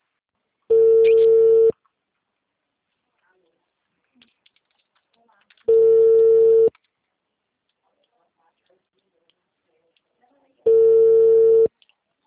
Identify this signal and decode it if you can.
Ringback tone (China, 450 Hz), with speech.